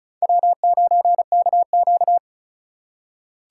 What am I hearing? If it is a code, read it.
Morse code, heard as a keyed tone, about 36 wpm, 700 Hz. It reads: W9KQ
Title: USA Call Signs - 4 Characters - 3x Character Spacing - 35wpm